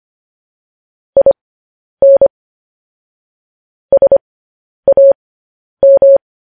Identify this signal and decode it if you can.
Morse code, heard as a keyed tone, about 25 wpm, spaced as for 10 wpm, 550 Hz. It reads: IN SAM
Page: News Headlines in Morse Code at 10 WPM